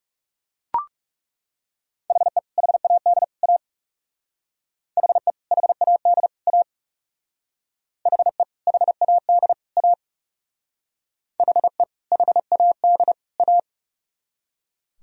Morse code, heard as a keyed tone, about 34 wpm, 700 Hz. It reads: HE HAD A HE HAD A HE HAD A HE HAD A
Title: Speed-Racing - Sets of 3 Words 30wpm